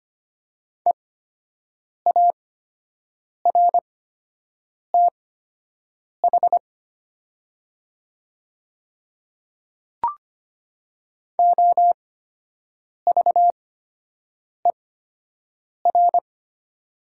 Morse code, heard as a keyed tone, about 25 wpm, spaced as for 7 wpm, 700 Hz. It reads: EARTH OVER
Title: 8x Character Spacing - Top 500 Words - 25wpm